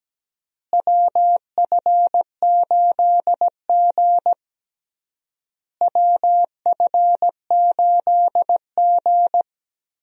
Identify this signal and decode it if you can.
Morse code, heard as a keyed tone, about 17 wpm, 700 Hz. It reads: WF8G WF8G
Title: USA Call Signs - 4 Characters - Repeated 3x - 17wpm